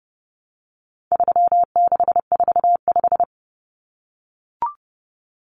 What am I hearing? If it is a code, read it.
Morse code, heard as a keyed tone, about 28 wpm, 700 Hz. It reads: 3645
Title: Common Cut Numbers - 4-Digit 30wpm